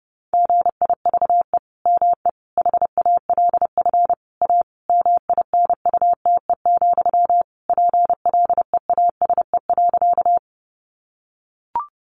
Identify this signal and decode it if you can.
Morse code, heard as a keyed tone, about 30 wpm, 700 Hz. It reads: GIVE ME HALF A MINUTE, PLEASE.
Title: Sentences from Top 1500 Words 30wpm